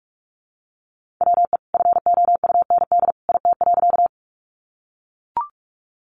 Morse code, heard as a keyed tone, about 44 wpm, 700 Hz. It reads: WE FOUND IT.
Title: Sentences from Top 300 Words 45wpm